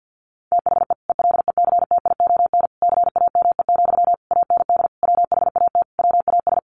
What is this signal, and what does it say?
Morse code, heard as a keyed tone, about 50 wpm, 700 Hz. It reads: THE ELECTION CAME, AND WHAT WAS
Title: Sentences from Top 1500 Words 50wpm